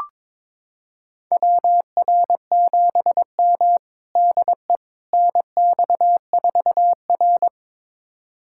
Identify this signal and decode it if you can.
Morse code, heard as a keyed tone, about 22 wpm, 700 Hz. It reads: WR7M DE NX4R
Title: USA Callsign DE Callsign - 4 Characters - 22wpm